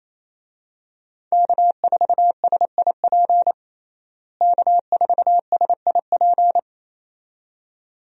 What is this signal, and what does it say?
Morse code, heard as a keyed tone, about 28 wpm, 700 Hz. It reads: K4SIP K4SIP
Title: USA Call Signs - 5 Characters - Repeated 2x - 28wpm